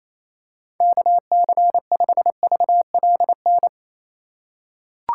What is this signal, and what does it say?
Morse code, heard as a keyed tone, about 27 wpm, 700 Hz. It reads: KC5VLN
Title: USA Call Signs - 6 Characters - 4x Character Spacing - 28wpm